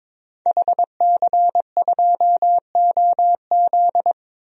Morse code, heard as a keyed tone, about 22 wpm, 700 Hz. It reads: HC2OZ
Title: DXCC-100 Call Signs - 5 Characters - 22wpm